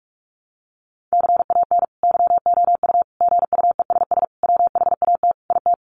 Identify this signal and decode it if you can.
Morse code, heard as a keyed tone, about 44 wpm, 700 Hz. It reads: CAN YOU GUESS WHAT IT
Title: Sentences from Top 1500 Words 45wpm